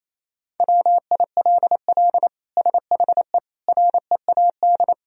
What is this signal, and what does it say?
Morse code, heard as a keyed tone, about 28 wpm, 700 Hz. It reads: WILL SHE READ
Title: Sentences from Top 1000 Words - Review Part 4 - 28wpm